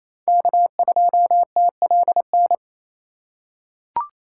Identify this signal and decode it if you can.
Morse code, heard as a keyed tone, about 28 wpm, 700 Hz. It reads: K2TLN
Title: USA Call Signs - 4x Character Spacing - 28wpm